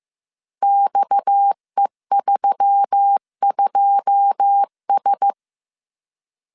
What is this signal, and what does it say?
Morse code, heard as a keyed tone, about 15 wpm, 800 Hz. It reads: XE32S
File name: CW / DET / 1 9 1500C S C 14wpm 3245